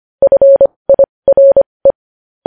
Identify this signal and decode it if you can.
Morse code, heard as a keyed tone, about 25 wpm, spaced as for 20 wpm, 550 Hz. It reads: FIRE
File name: podcast / News headlines-1x-25x20